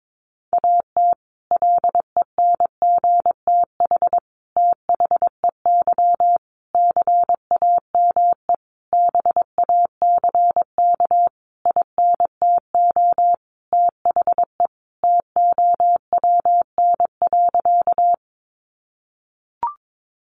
Morse code, heard as a keyed tone, about 22 wpm, 700 Hz. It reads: AT LENGTH THEY CAME BACK INTO THE TOWN.
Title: Sentences from Top 1000 Words 22wpm